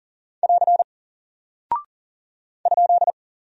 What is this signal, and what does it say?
Morse code, heard as a keyed tone, about 40 wpm, 700 Hz. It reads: <AR> ?
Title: Single Character (Letter:Number:Punctuation:Prosign) Alt 40wpm